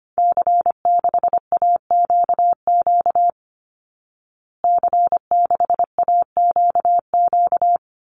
Morse code, heard as a keyed tone, about 25 wpm, 700 Hz. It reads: C6AQQ C6AQQ
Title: Call Signs Repeated 2x - 25wpm